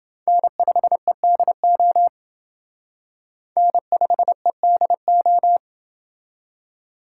Morse code, heard as a keyed tone, about 28 wpm, 700 Hz. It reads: N5EDO N5EDO